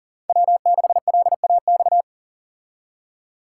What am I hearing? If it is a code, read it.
Morse code, heard as a keyed tone, about 40 wpm, 700 Hz. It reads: W6LAX
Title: K1USN SST QSO Practice - 40wpm